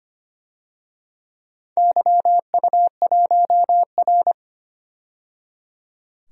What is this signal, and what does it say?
Morse code, heard as a keyed tone, about 25 wpm, 700 Hz. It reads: YU1R
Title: DXCC-100 Call Signs - 4 Characters - 25wpm